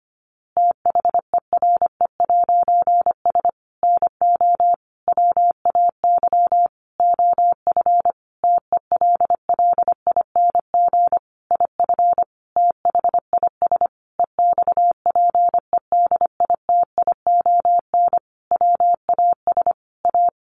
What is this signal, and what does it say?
Morse code, heard as a keyed tone, about 25 wpm, 700 Hz. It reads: THERE'S NO WAY OF TELLING IF THIS EXPEDITION WAS A
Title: A Sound Of Thunder 25wpm